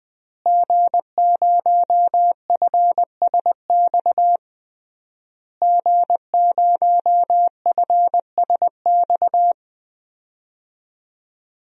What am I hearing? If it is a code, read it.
Morse code, heard as a keyed tone, about 21 wpm, 700 Hz. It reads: G0FSX G0FSX
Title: DXCC-100 Call Signs - 5 Characters - Repeated 3x - 20wpm